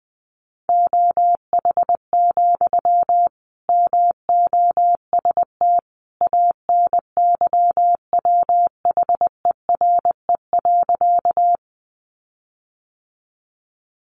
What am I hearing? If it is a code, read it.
Morse code, heard as a keyed tone, about 20 wpm, 700 Hz. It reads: OH, MOST ANYWHERE.
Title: Sentences from Top 1400 Words 20wpm